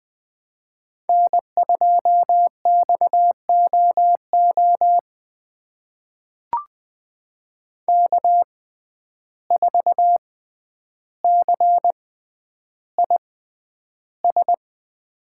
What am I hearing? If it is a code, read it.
Morse code, heard as a keyed tone, about 20 wpm, spaced as for 10 wpm, 700 Hz. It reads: N2XOO K4CIS
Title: USA Call Signs - 5 Characters - 6x Character Spacing - 20wpm